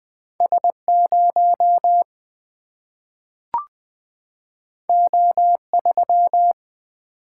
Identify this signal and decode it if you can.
Morse code, heard as a keyed tone, about 20 wpm, 700 Hz. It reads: S 0 O 3
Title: Single-Letter Single-Number 20wpm